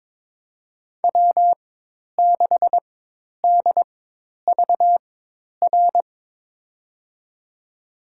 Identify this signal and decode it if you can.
Morse code, heard as a keyed tone, about 23 wpm, spaced as for 10 wpm, 700 Hz. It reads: W6DVR
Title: USA Call Signs - 4x Character Spacing - 22wpm